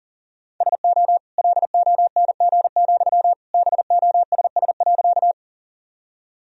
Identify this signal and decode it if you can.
Morse code, heard as a keyed tone, about 40 wpm, 690 Hz. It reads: SO LONG, BOSS.
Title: Sentences from Top 1000 Words - Review Part 4 - 40wpm